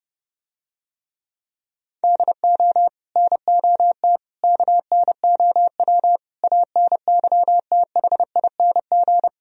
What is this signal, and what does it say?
Morse code, heard as a keyed tone, about 29 wpm, 700 Hz. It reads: DO NOT KNOW ANYTHING